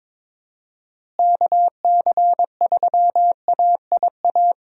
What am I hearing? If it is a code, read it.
Morse code, heard as a keyed tone, about 22 wpm, 700 Hz. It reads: KC3AIA
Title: USA Call Signs - 6 Characters - Repeated 3x - 22wpm